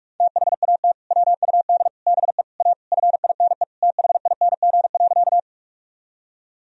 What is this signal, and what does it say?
Morse code, heard as a keyed tone, about 46 wpm, 700 Hz. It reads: THAT WUD BE A FINE THING.
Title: Sentences from Top 500 Words and Abbreviations 45wpm